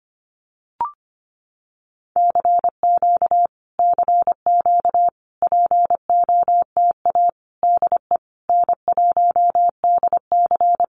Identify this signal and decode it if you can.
Morse code, heard as a keyed tone, about 25 wpm, 700 Hz. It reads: CQ CQ POTA DE N1DC
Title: Common QSO Phrases Encore 25wpm